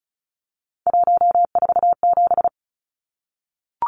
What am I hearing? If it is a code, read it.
Morse code, heard as a keyed tone, about 34 wpm, 700 Hz. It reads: 147
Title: Numbers - 3-Digit 35wpm